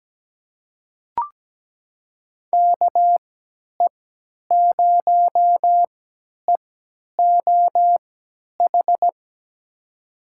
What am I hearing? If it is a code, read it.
Morse code, heard as a keyed tone, about 17 wpm, spaced as for 10 wpm, 700 Hz. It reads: KE0EOH
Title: USA Call Signs - 6 Characters - 3x Character Spacing - 17wpm